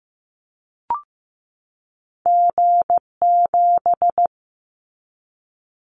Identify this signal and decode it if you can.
Morse code, heard as a keyed tone, about 15 wpm, 700 Hz. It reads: G7
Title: Callsign Prefixes for Top 25 DXCC - 15wpm